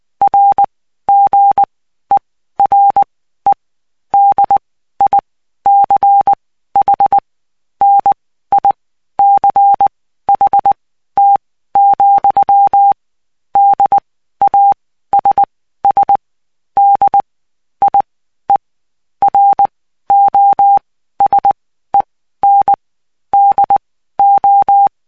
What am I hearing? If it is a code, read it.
Morse code, heard as a keyed tone, about 20 wpm, spaced as for 13 wpm, 800 Hz. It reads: RGERE DICH NICHT, DASS DIE ROSEN DO
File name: cw 75BPM